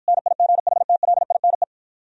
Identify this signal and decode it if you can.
Morse code, heard as a keyed tone, about 52 wpm, 700 Hz. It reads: NIGHTLINE